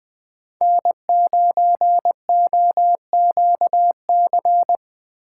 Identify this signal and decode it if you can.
Morse code, heard as a keyed tone, about 20 wpm, 700 Hz. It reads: N9OQC
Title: Speed Racing - USA Call Signs 17wpm